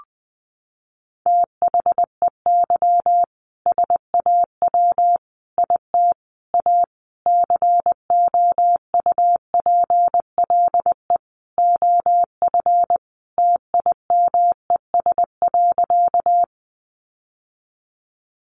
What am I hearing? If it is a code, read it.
Morse code, heard as a keyed tone, about 20 wpm, 700 Hz. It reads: THEY SAW IT A COUPLE OF TIMES.
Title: Sentences from Top 1100 Words 20wpm